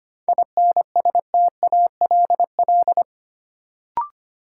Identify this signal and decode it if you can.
Morse code, heard as a keyed tone, about 25 wpm, 700 Hz. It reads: INSTALL
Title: Sentences from Top 1600 Words - New Word Drill - 25wpm